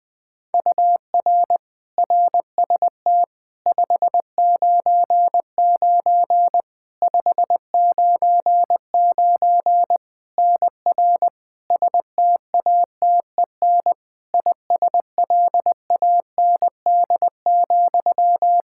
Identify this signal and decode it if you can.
Morse code, heard as a keyed tone, about 20 wpm, 700 Hz. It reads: UR RST 599 599 NR STATEN ISLAND,
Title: Practice QSO - Exchange 1 - Beginning - 20wpm